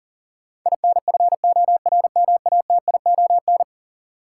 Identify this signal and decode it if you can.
Morse code, heard as a keyed tone, about 40 wpm, 700 Hz. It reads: INFORMATION
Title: Sentences from Top 600 Words - New Word Drill - 40wpm